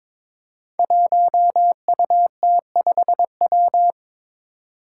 Morse code, heard as a keyed tone, about 22 wpm, 700 Hz. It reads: 1UT5W
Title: Five Challenging Letter-Number Combinations 22wpm